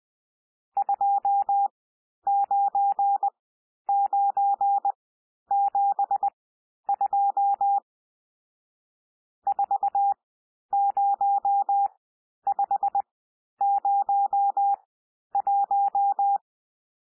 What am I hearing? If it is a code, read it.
Morse code, heard as a keyed tone, about 20 wpm, spaced as for 11 wpm, 800 Hz. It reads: 29972 40501
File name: cwmp3 / N1 SOUND NUMBERS 20 wpm